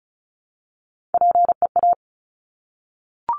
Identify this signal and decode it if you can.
Morse code, heard as a keyed tone, about 33 wpm, 700 Hz. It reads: PEA
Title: Top Three Letter Combinations in Words 35wpm